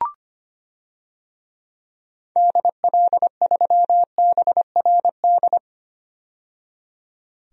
Morse code, heard as a keyed tone, about 25 wpm, 700 Hz. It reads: DL3BRD
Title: DXCC-100 Call Signs - 6 Characters - 25wpm